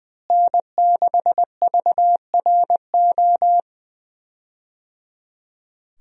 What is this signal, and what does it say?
Morse code, heard as a keyed tone, about 21 wpm, 700 Hz. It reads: N6VRO
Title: USA Call Signs - 5 Characters - Repeated 3x - 20wpm